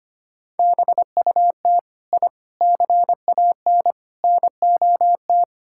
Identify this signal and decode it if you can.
Morse code, heard as a keyed tone, about 25 wpm, 700 Hz. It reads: BUT I CAN NOT
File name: practice / 25wpm / Sentences with 4-Word Phrases - 25wpm